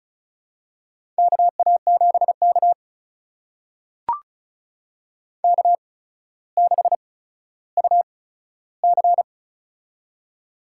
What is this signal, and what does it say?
Morse code, heard as a keyed tone, about 36 wpm, spaced as for 15 wpm, 700 Hz. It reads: KA7K K6UC
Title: USA Call Signs - 4 Characters - 8x Character Spacing - 35wpm